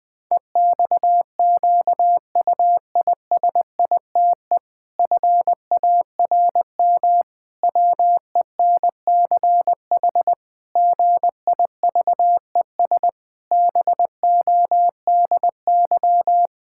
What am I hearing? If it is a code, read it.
Morse code, heard as a keyed tone, about 20 wpm, 700 Hz. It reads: EXQUISITE FARM WENCH GIVES BODY
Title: Pangram Challenge v2 - 20wpm